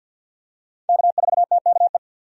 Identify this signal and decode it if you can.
Morse code, heard as a keyed tone, about 52 wpm, 700 Hz. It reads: K4TKE
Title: USA Call Signs - 5 Characters - Repeated 3x - 50wpm